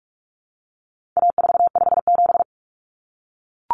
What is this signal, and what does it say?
Morse code, heard as a keyed tone, about 44 wpm, 700 Hz. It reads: A457
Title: Common Cut Numbers - 4-Digit 45wpm